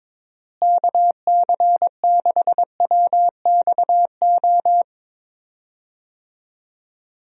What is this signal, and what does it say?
Morse code, heard as a keyed tone, about 22 wpm, 700 Hz. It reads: KC6WXO
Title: USA Call Signs - 6 Characters - Repeated 2x - 22wpm